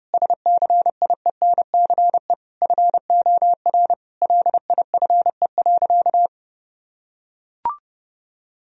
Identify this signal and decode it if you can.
Morse code, heard as a keyed tone, about 32 wpm, 700 Hz. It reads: SCIENCE FOR LIFE.
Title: Sentences from Top 1000 Words - Review Part 2 - 30wpm